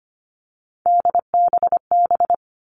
Morse code, heard as a keyed tone, about 25 wpm, 700 Hz. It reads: DBB
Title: All Cut Numbers - 3-Digit 25wpm